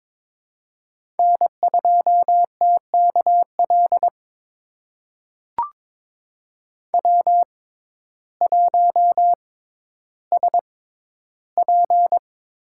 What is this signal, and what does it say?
Morse code, heard as a keyed tone, about 22 wpm, spaced as for 12 wpm, 700 Hz. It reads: N2TKL W1SP
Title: USA Call Signs - 5 Characters - 6x Character Spacing - 22wpm